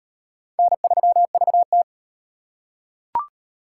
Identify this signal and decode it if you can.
Morse code, heard as a keyed tone, about 35 wpm, 700 Hz. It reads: N3VT